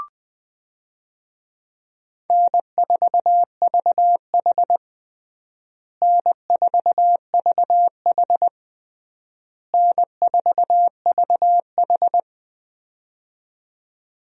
Morse code, heard as a keyed tone, about 21 wpm, 700 Hz. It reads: N4VH N4VH N4VH
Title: USA Call Signs - 4 Characters - Repeated 3x - 20wpm